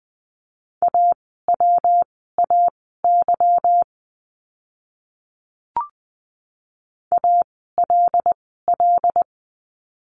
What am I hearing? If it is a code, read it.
Morse code, heard as a keyed tone, about 20 wpm, spaced as for 15 wpm, 700 Hz. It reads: AWAY ALL
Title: 146 Lesson 24f - Letter y - Words 20wpm